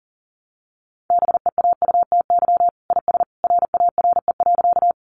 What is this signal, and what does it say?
Morse code, heard as a keyed tone, about 40 wpm, 700 Hz. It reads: BEAUTY IS RARE.
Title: Sentences from Top 1100 Words 40wpm